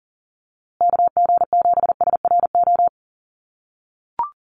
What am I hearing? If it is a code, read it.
Morse code, heard as a keyed tone, about 39 wpm, 700 Hz. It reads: KG7SRO
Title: USA Call Signs Repeated 3x 40wpm